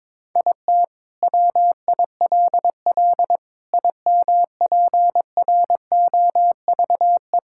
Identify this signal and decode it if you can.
Morse code, heard as a keyed tone, about 22 wpm, 700 Hz. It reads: IT WILL IMPROVE
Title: Sentences from Top 1000 Words - Review Part 5 - 22wpm